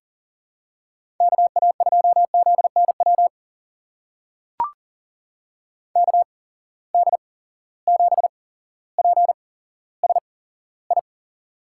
Morse code, heard as a keyed tone, about 40 wpm, spaced as for 17 wpm, 700 Hz. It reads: KA2ZNW KD7PSI
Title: USA Call Signs - 6 Characters - 8x Character Spacing - 40wpm